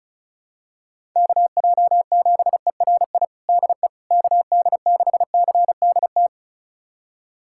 Morse code, heard as a keyed tone, about 36 wpm, 700 Hz. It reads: KJ7ERI DE KD6CDT
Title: USA Callsign DE Callsign 35wpm